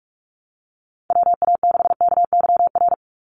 Morse code, heard as a keyed tone, about 44 wpm, 700 Hz. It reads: WA6KYR
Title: NA SOTA Chaser Callsigns 45wpm